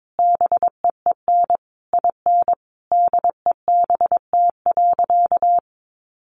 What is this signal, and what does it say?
Morse code, heard as a keyed tone, about 22 wpm, 700 Hz. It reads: BEEN IN DEBT.
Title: Sentences from Top 1300 Words 22wpm